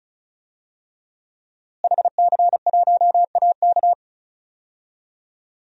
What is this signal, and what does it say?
Morse code, heard as a keyed tone, about 34 wpm, 700 Hz. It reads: HC1AK
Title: DXCC-100 Call Signs - 5 Characters - Repeated 2x - 35wpm